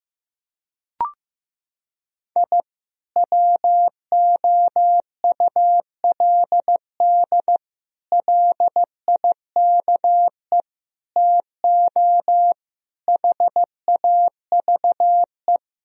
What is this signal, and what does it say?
Morse code, heard as a keyed tone, about 15 wpm, 700 Hz. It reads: I WOULD LIKE TO HAVE
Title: Sentences from Top 1000 Words - Review Part 3 - 15wpm